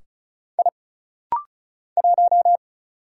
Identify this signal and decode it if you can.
Morse code, heard as a keyed tone, about 36 wpm, 700 Hz. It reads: I 1
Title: Single Letter-Number - Rapid-Fire 35wpm